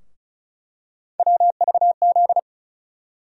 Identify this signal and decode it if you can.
Morse code, heard as a keyed tone, about 36 wpm, 700 Hz. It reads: WVZ
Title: USA Call Sign Postfixes - 3 Letters 35wpm